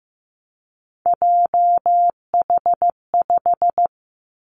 Morse code, heard as a keyed tone, about 15 wpm, 700 Hz. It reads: JH5
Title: Three Challenging Letter-Number Combinations 15wpm